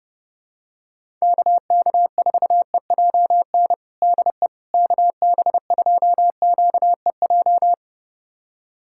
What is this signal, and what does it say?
Morse code, heard as a keyed tone, about 30 wpm, 700 Hz. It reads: KK4EJN DE KB2QEJ
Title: USA Callsign DE Callsign 30wpm